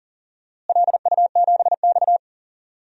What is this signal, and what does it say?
Morse code, heard as a keyed tone, about 37 wpm, 700 Hz. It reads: LU7X
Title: DXCC-100 Call Signs - 4 Characters - Repeated 3x - 40wpm